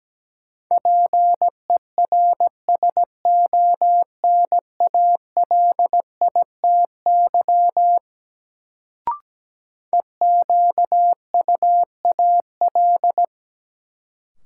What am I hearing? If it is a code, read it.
Morse code, heard as a keyed tone, about 17 wpm, 700 Hz. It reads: PERSONALITY EQUAL
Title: Sentences from Top 1400 Words - New Word Drill Alt - 17wpm